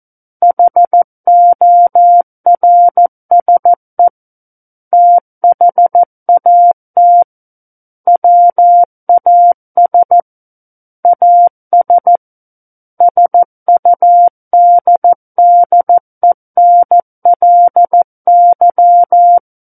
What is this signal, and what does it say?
Morse code, heard as a keyed tone, about 14 wpm, 700 Hz. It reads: HORSE THAT WAS AS SUDDENLY
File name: Book One / 05-Chapter 5 - 14 WPM